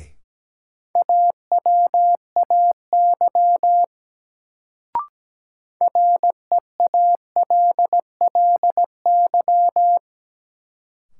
Morse code, heard as a keyed tone, about 17 wpm, 700 Hz. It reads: AWAY REALLY
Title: Sentences from Top 600 Words - New Word Drill - 17wpm